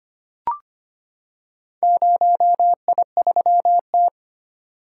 Morse code, heard as a keyed tone, about 25 wpm, 700 Hz. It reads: 0I3T
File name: practice / 25wpm / Four Letter-Number Combinations - 25wpm